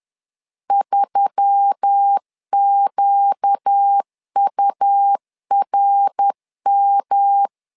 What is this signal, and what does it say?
Morse code, heard as a keyed tone, about 10 wpm, 800 Hz. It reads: 3QURM